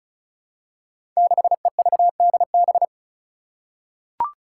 Morse code, heard as a keyed tone, about 34 wpm, 700 Hz. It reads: 6EVDB
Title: Five Challenging Letter-Number Combinations 35wpm